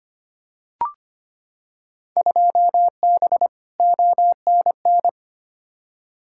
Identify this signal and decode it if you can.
Morse code, heard as a keyed tone, about 25 wpm, 700 Hz. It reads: 2B ONN
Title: Field Day Exchanges - 25wpm